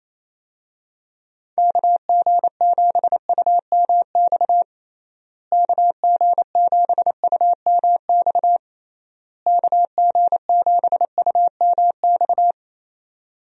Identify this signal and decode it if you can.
Morse code, heard as a keyed tone, about 28 wpm, 700 Hz. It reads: KG7UMX KG7UMX KG7UMX
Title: USA Call Signs - 6 Characters - Repeated 3x - 28wpm